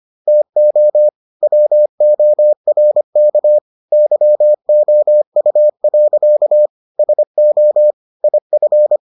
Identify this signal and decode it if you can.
Morse code, heard as a keyed tone, about 25 wpm, 600 Hz. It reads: TO WORK YOU. SO IF